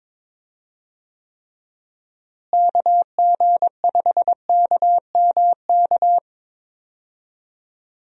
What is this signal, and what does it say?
Morse code, heard as a keyed tone, about 22 wpm, 700 Hz. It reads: KG5KMK
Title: USA Call Signs - 6 Characters - Repeated 2x - 22wpm